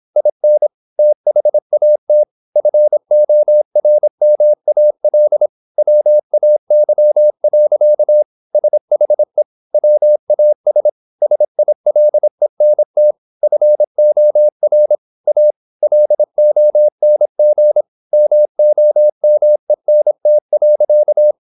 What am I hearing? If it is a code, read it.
Morse code, heard as a keyed tone, about 26 wpm, 600 Hz. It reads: IN THAT FORMAL WAY. SHE WAS SILENT FOR A LONG MOMENT.
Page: Never 26wpm | CW med Gnister